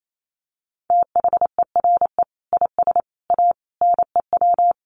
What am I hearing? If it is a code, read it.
Morse code, heard as a keyed tone, about 28 wpm, 700 Hz. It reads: THERE IS A NEW
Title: Sentences from Top 1100 Words 28wpm